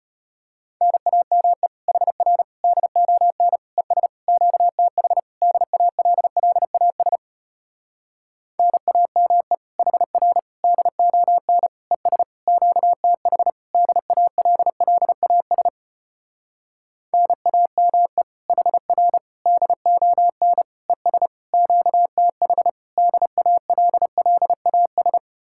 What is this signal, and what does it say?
Morse code, heard as a keyed tone, about 33 wpm, 700 Hz. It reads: NAME HR DON ES QTH DALLAS NAME HR DON ES QTH DALLAS NAME HR DON ES QTH DALLAS